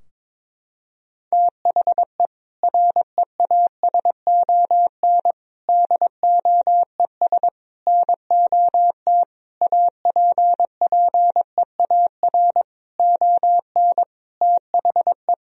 Morse code, heard as a keyed tone, about 22 wpm, 700 Hz. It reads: THE REASON DOES NOT APPEAR ON THE
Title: Sentences from Top 1000 Words - Review Part 3 - 22wpm